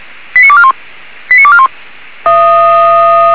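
Five-tone selective call (ZVEI), heard as quick runs of five tones, tones of about 70 ms. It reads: 89241 89241
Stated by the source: ZVEI five-tone selective call received on an SDR.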